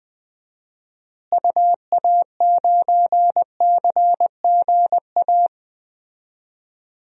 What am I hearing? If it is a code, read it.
Morse code, heard as a keyed tone, about 20 wpm, 700 Hz. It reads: UA9CGA
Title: DXCC-100 Call Signs - 6 Characters - Repeated 3x - 20wpm